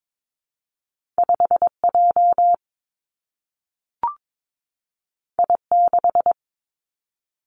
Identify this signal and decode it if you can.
Morse code, heard as a keyed tone, about 22 wpm, 700 Hz. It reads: 5 J I 6
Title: Two Challenging Letter-Number Combinations 22wpm